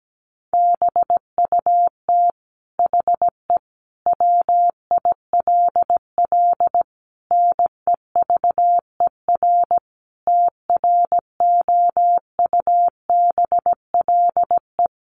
Morse code, heard as a keyed tone, about 17 wpm, 700 Hz. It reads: BUT HE WILL NEVER TROUBLE
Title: Sentences from Top 1200 Words 17wpm